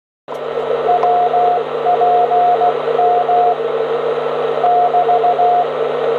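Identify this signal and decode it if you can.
Morse code, heard as a keyed tone, about 20 wpm, spaced as for 16 wpm, 710 Hz, with radio noise. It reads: WPM <BT>
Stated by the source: 160m signal on 1854 kHz, 100W into a 60m antenna, received on an R-311 with an indoor antenna and microphone, 400Hz filter, noise S5.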